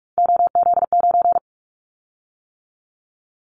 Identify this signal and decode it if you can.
Morse code, heard as a keyed tone, about 44 wpm, 700 Hz. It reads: OZ9
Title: Callsign Prefixes for Top 100 DXCC - 45wpm